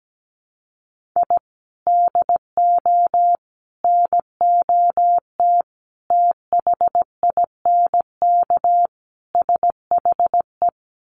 Morse code, heard as a keyed tone, about 17 wpm, 700 Hz. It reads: I DO NOT THINK SHE
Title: Sets of 5 Words 17wpm